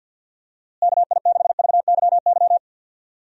Morse code, heard as a keyed tone, about 52 wpm, 700 Hz. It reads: KI6VYY
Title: USA Call Signs - 6 Characters - Repeated 3x - 50wpm